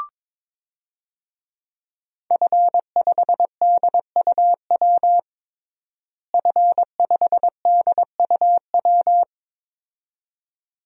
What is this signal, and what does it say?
Morse code, heard as a keyed tone, about 21 wpm, 700 Hz. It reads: F5DUW F5DUW
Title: DXCC-100 Call Signs - 5 Characters - Repeated 2x - 22wpm